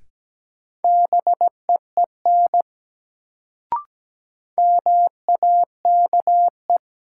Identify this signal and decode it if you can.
Morse code, heard as a keyed tone, about 17 wpm, 700 Hz. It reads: BEEN MAKE
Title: Sentences from Top 100 Words - New Word Drill - 17wpm